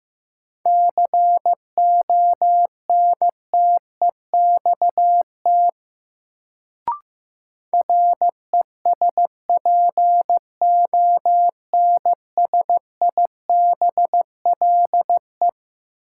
Morse code, heard as a keyed tone, about 15 wpm, 700 Hz. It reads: CONTEXT RESPONSIBLE
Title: Sentences from Top 1000 Words - Review Part 7 - New Word Drill - 15wpm